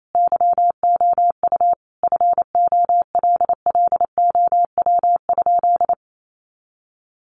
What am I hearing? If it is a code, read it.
Morse code, heard as a keyed tone, about 28 wpm, 700 Hz. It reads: YOU FOLLOW?
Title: Sentences from Top 1100 Words 28wpm